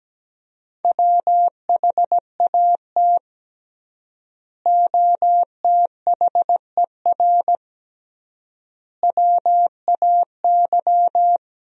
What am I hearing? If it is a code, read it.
Morse code, heard as a keyed tone, about 17 wpm, 700 Hz. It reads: WHAT OTHER WAY
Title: 3x Word-Spacing - Sentences from Top 100 Words - 17wpm